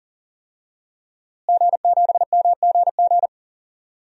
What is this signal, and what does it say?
Morse code, heard as a keyed tone, about 40 wpm, 700 Hz. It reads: G7MGG
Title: DXCC-100 Call Signs - 5 Characters - Repeated 2x - 40wpm